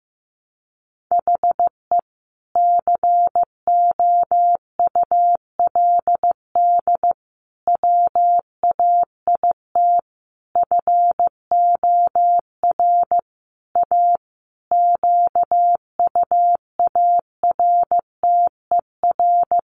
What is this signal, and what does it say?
Morse code, heard as a keyed tone, about 15 wpm, 700 Hz. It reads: HE COULD WAIT FOR A QUARTER
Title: Sentences from Top 1300 Words 15wpm